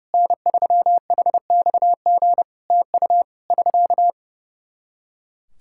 Morse code, heard as a keyed tone, about 30 wpm, 700 Hz. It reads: N3HXG TU <SK>
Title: Practice QSO - Exchange 4 - Ending - 30wpm